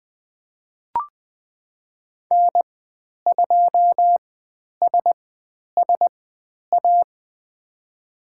Morse code, heard as a keyed tone, about 20 wpm, spaced as for 10 wpm, 700 Hz. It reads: N2SSA
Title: 109 Lesson 18g - Letter m - Call Signs 20wpm